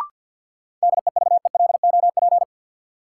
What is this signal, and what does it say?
Morse code, heard as a keyed tone, about 48 wpm, 700 Hz. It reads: DEVELOP
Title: Sentences from Top 1000 Words - Review Part 2 - New Word Drill - 50wpm